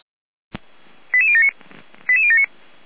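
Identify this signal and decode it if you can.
Five-tone selective call (ZVEI), heard as quick runs of five tones, tones of about 70 ms. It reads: 88079 88079
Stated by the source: ZVEI five-tone call received by SDR.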